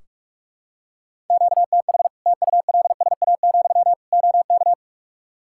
Morse code, heard as a keyed tone, about 46 wpm, 700 Hz. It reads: QTH TULSA, OK
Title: Practice QSO - Exchange 1 - Beginning - 45wpm